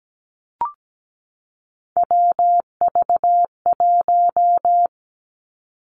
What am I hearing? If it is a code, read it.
Morse code, heard as a keyed tone, about 17 wpm, 700 Hz. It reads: WV1
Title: Three Challenging Letter-Number Combinations 17wpm